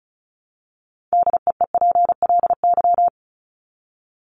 Morse code, heard as a keyed tone, about 34 wpm, 700 Hz. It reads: DEEPLY
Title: LY Suffix - Top 100 Words with Base Words 35wpm